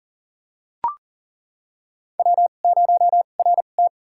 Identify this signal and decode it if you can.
Morse code, heard as a keyed tone, about 37 wpm, spaced as for 28 wpm, 700 Hz. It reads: W0RT
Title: USA Call Signs - 4 Characters - 2x Character Spacing - 40wpm